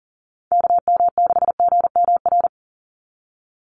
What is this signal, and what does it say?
Morse code, heard as a keyed tone, about 40 wpm, 700 Hz. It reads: KM6GMR
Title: USA Call Signs 40wpm